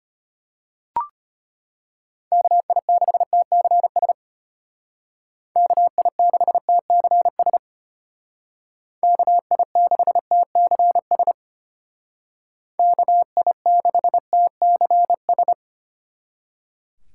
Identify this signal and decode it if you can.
Morse code, heard as a keyed tone, about 32 wpm, 700 Hz. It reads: KI6TCS KI6TCS KI6TCS KI6TCS